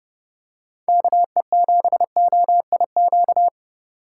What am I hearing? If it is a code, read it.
Morse code, heard as a keyed tone, about 30 wpm, 700 Hz. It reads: KE7OIQ
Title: USA Call Signs - 6 Characters - 2x Character Spacing - 30wpm